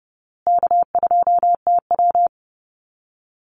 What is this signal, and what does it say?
Morse code, heard as a keyed tone, about 30 wpm, 700 Hz. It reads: K2TW
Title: Easy Call Signs Repeated 3x - 30wpm